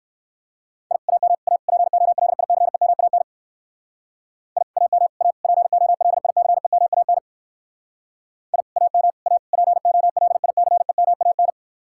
Morse code, heard as a keyed tone, about 58 wpm, 700 Hz. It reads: I AM A POLICEMAN I AM A POLICEMAN I AM A POLICEMAN